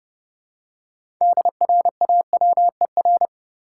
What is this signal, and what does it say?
Morse code, heard as a keyed tone, about 30 wpm, 700 Hz. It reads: DRAWER
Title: Six Letter Words - 30wpm